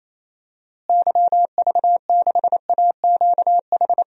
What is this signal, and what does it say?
Morse code, heard as a keyed tone, about 28 wpm, 700 Hz. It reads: YV6AQH
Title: DXCC-100 Call Signs - 6 Characters - Repeated 2x - 28wpm